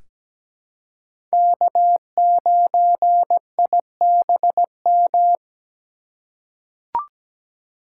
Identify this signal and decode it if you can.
Morse code, heard as a keyed tone, about 17 wpm, 700 Hz. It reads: K9IBM
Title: K1USN SST QSO Practice - 17wpm